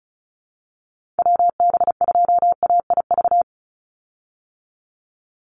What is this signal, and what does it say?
Morse code, heard as a keyed tone, about 34 wpm, 700 Hz. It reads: WB2AIV
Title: Call Signs Repeated 3x - 35wpm